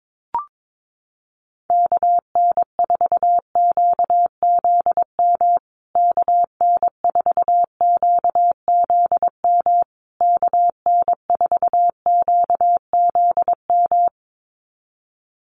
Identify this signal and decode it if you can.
Morse code, heard as a keyed tone, about 22 wpm, 700 Hz. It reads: KN4QZM KN4QZM KN4QZM
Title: USA Call Signs Repeated 3x 22wpm